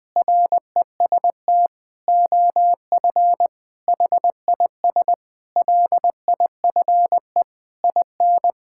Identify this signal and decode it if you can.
Morse code, heard as a keyed tone, about 21 wpm, 700 Hz. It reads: REST OF HIS LIFE IN
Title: Sentences with 8-Word Phrases - 20wpm